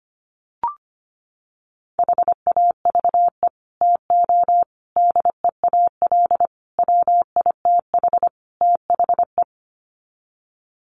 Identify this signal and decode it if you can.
Morse code, heard as a keyed tone, about 25 wpm, 700 Hz. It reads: HAVE TO DEAL WITH THE
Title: Sets of 5 Words Encore2 25wpm